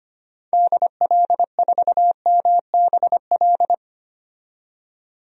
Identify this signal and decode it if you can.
Morse code, heard as a keyed tone, about 25 wpm, 700 Hz. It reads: DL4MBL
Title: DXCC-100 Call Signs - 6 Characters - 25wpm